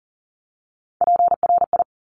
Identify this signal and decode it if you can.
Morse code, heard as a keyed tone, about 41 wpm, 700 Hz. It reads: PRI
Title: Top Three Letter Combinations in Words 40wpm